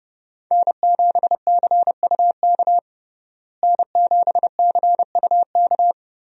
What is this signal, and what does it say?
Morse code, heard as a keyed tone, about 30 wpm, 700 Hz. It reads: N7CUK N7CUK
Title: USA Call Signs - 5 Characters - Repeated 3x - 30wpm